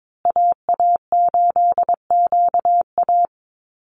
Morse code, heard as a keyed tone, about 22 wpm, 700 Hz. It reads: AA8QA
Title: USA Call Signs Repeated 3x 22wpm